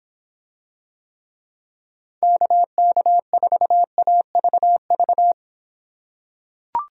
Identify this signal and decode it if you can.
Morse code, heard as a keyed tone, about 26 wpm, 700 Hz. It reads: KK4AVV